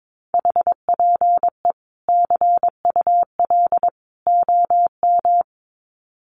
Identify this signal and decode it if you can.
Morse code, heard as a keyed tone, about 22 wpm, 700 Hz. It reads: HPE CUL OM
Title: Common QSO Phrases 22wpm